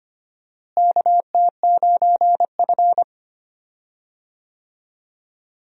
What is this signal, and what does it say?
Morse code, heard as a keyed tone, about 25 wpm, 700 Hz. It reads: KT9F
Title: USA Call Signs - 4 Characters - Repeated 2x - 25wpm